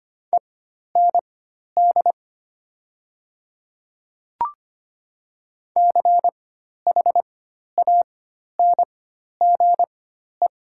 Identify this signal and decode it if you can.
Morse code, heard as a keyed tone, about 25 wpm, spaced as for 12 wpm, 700 Hz. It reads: END CHANGE
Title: 4x Character Spacing - Top 200 Words - 25wpm